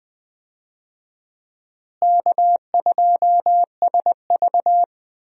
Morse code, heard as a keyed tone, about 20 wpm, 700 Hz. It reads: K2SV
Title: USA Call Signs - 4 Characters - Repeated 3x - 20wpm